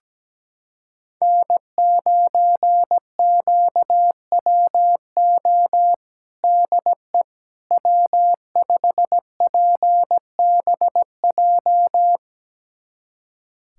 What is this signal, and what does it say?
Morse code, heard as a keyed tone, about 17 wpm, 700 Hz. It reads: N9QWO DE W5PBJ
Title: USA Callsign DE Callsign - 5 Characters - 17wpm